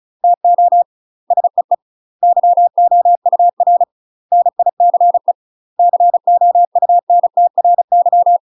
Morse code, heard as a keyed tone, about 34 wpm, 700 Hz. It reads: TO SEE YOUR NICE COUNTRY
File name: UK MASTER QSOs TRAINING/35WPM